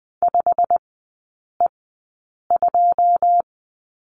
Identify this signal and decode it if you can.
Morse code, heard as a keyed tone, about 20 wpm, spaced as for 8 wpm, 700 Hz. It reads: 5E2
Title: 107 Lesson 18e - Letter m - Three Characters 20wpm